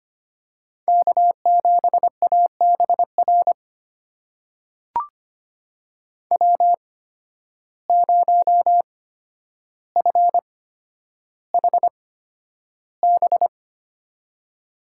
Morse code, heard as a keyed tone, about 25 wpm, spaced as for 11 wpm, 700 Hz. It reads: K7ABR W0FHB
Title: USA Call Signs - 5 Characters - 8x Character Spacing - 25wpm